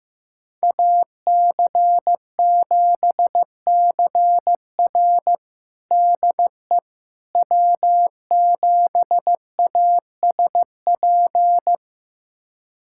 Morse code, heard as a keyed tone, about 16 wpm, 700 Hz. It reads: AC7CR DE W7ASP
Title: USA Callsign DE Callsign - 5 Characters - 15wpm